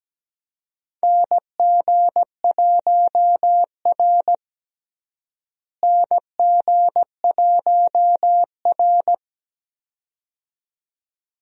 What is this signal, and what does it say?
Morse code, heard as a keyed tone, about 17 wpm, 700 Hz. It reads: NG1R NG1R
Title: USA Call Signs - 4 Characters - Repeated 2x - 17wpm